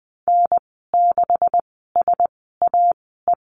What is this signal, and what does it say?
Morse code, heard as a keyed tone, about 20 wpm, spaced as for 15 wpm, 700 Hz. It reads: N6SAE
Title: 128 Lesson 21g - Number 6 - Call Signs 20wpm